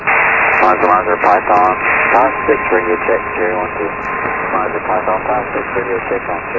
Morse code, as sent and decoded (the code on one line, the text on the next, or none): none